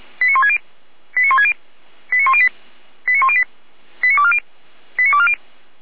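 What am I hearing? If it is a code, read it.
Five-tone selective call (ZVEI), heard as quick runs of five tones, tones of about 70 ms. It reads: 89160 89160 89108 89108 89240 89240
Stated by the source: Five-tone ZVEI selective call received on SDR.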